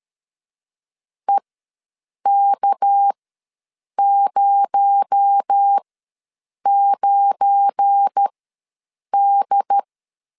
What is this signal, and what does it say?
Morse code, heard as a keyed tone, about 13 wpm, spaced as for 7 wpm, 800 Hz. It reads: EK09D
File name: LEZIONI / 7 GF9Z / 7 GF9 ws 12wpm 3344